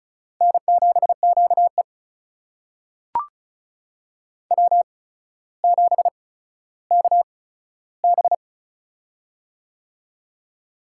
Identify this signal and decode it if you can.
Morse code, heard as a keyed tone, about 36 wpm, spaced as for 15 wpm, 700 Hz. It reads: N7QE W7KB
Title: USA Call Signs - 4 Characters - 8x Character Spacing - 35wpm